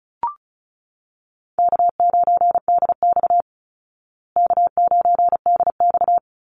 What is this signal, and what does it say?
Morse code, heard as a keyed tone, about 34 wpm, 700 Hz. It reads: K9DX K9DX
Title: Easy Call Signs Repeated 2x - 35wpm